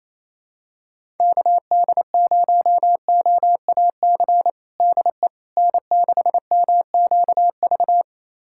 Morse code, heard as a keyed tone, about 28 wpm, 700 Hz. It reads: KD0OAC DE N6MQV
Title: USA Callsign DE Callsign 28wpm